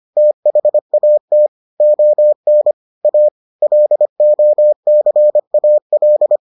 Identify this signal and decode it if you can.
Morse code, heard as a keyed tone, about 25 wpm, 600 Hz. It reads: THAT ON A LOCAL